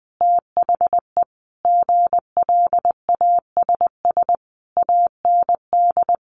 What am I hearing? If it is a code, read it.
Morse code, heard as a keyed tone, about 20 wpm, 700 Hz. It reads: THE GLASS AND
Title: Sentences from Top 1300 Words 20wpm